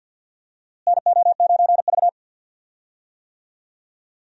Morse code, heard as a keyed tone, about 51 wpm, 700 Hz. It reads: NO9V
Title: USA Call Signs - 4 Characters - Repeated 2x - 50wpm